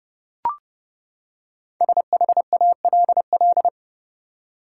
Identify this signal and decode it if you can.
Morse code, heard as a keyed tone, about 30 wpm, 700 Hz. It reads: SHALL
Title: Five Letter Words - 30wpm